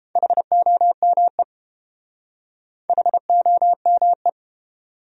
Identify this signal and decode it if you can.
Morse code, heard as a keyed tone, about 31 wpm, 700 Hz. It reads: HOME HOME